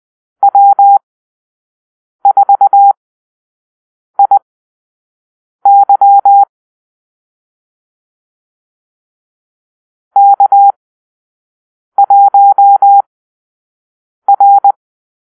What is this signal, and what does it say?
Morse code, heard as a keyed tone, about 20 wpm, spaced as for 6 wpm, 800 Hz. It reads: W4IY K1R
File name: arrl / Call-Sign Drill-5